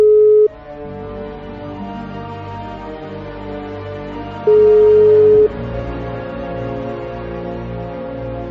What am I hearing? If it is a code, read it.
Ringback tone (Europe, 425 Hz).